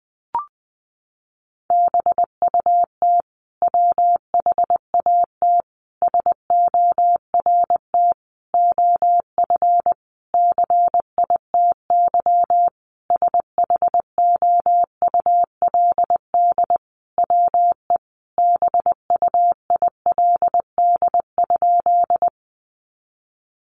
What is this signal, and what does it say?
Morse code, heard as a keyed tone, about 20 wpm, 700 Hz. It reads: BUT WHAT SORT OF CITY SHOULD WE BUILD?
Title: Sentences from Top 1300 Words 20wpm